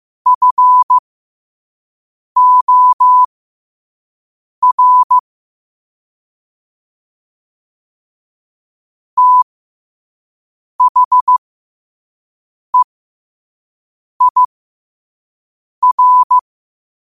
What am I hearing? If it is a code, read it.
Morse code, heard as a keyed tone, about 15 wpm, spaced as for 5 wpm, 1000 Hz. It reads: FOR THEIR